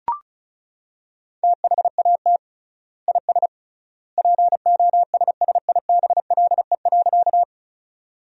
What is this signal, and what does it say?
Morse code, heard as a keyed tone, about 36 wpm, 700 Hz. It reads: THAT IS POSSIBLE.
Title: 3x Word-Spacing - Sentences from Top 500 Words - 35wpm